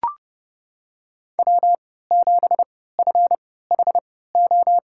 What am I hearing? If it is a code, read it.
Morse code, heard as a keyed tone, about 30 wpm, spaced as for 17 wpm, 700 Hz. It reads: W7FHO
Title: USA Call Signs - 5 Characters - 3x Character Spacing - 30wpm